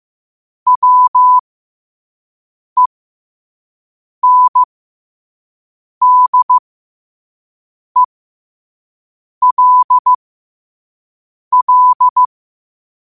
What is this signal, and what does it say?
Morse code, heard as a keyed tone, about 15 wpm, spaced as for 5 wpm, 1000 Hz. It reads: WENDELL
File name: qotdfiles / 5wpm